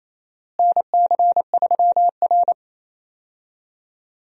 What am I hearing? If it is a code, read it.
Morse code, heard as a keyed tone, about 28 wpm, 700 Hz. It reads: NC3R
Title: USA Call Signs - 4 Characters - Repeated 2x - 28wpm